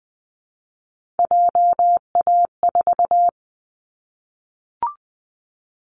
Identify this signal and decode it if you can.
Morse code, heard as a keyed tone, about 20 wpm, 700 Hz. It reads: JA4
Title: Callsign Prefixes for Top 25 DXCC - 20wpm